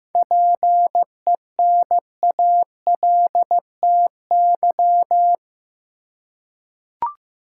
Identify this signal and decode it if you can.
Morse code, heard as a keyed tone, about 15 wpm, 700 Hz. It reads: PENALTY
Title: Seven Letter Words - 15wpm